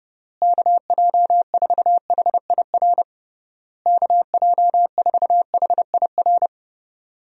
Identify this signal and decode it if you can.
Morse code, heard as a keyed tone, about 30 wpm, 700 Hz. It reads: KJ4HIR KJ4HIR
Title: USA Call Signs - 6 Characters - Repeated 3x - 30wpm